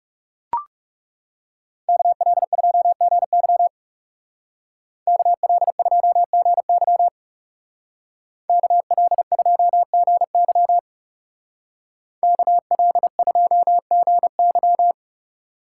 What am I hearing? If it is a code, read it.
Morse code, heard as a keyed tone, about 36 wpm, 700 Hz. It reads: KL2GY KL2GY KL2GY KL2GY
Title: Speed Racing - USA 2x2 Call Signs 30wpm